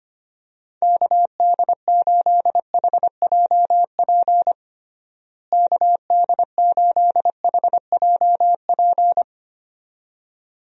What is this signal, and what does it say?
Morse code, heard as a keyed tone, about 25 wpm, 700 Hz. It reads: KD8HJP KD8HJP
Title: USA Call Signs - 6 Characters - Repeated 2x - 25wpm